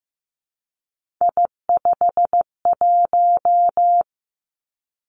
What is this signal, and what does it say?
Morse code, heard as a keyed tone, about 15 wpm, 700 Hz. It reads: I51
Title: Three Challenging Letter-Number Combinations 15wpm